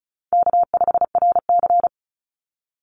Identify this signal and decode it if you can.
Morse code, heard as a keyed tone, about 34 wpm, 700 Hz. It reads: K5RC
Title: Easy Call Signs Repeated 2x - 35wpm